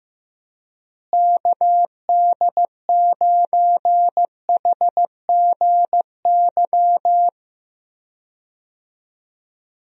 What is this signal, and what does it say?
Morse code, heard as a keyed tone, about 15 wpm, 700 Hz. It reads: KD9HGY
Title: USA Call Signs - 6 Characters - Repeated 2x - 15wpm